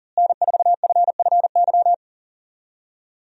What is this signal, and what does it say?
Morse code, heard as a keyed tone, about 40 wpm, 700 Hz. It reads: N4FFY
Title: USA Call Signs - 5 Characters - Repeated 3x - 40wpm